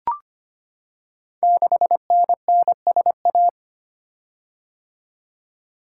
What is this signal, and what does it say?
Morse code, heard as a keyed tone, about 25 wpm, 700 Hz. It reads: 6NNSA
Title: Code Groups - 5 Letter:Numbers - 25wpm